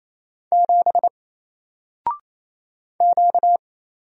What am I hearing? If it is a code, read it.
Morse code, heard as a keyed tone, about 27 wpm, 700 Hz. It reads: 7 Q
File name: Single-Letter-Number-28wpm